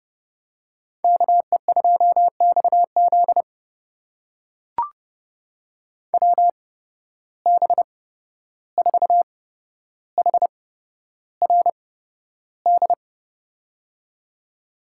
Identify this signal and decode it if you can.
Morse code, heard as a keyed tone, about 30 wpm, spaced as for 12 wpm, 700 Hz. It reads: KE2XZ WB4HRD
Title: USA Call Signs - 8x Character Spacing - 30wpm